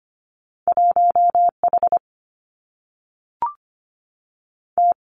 Morse code, heard as a keyed tone, about 25 wpm, 700 Hz. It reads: 1H T
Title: Two Challenging Letter-Number Combinations 25wpm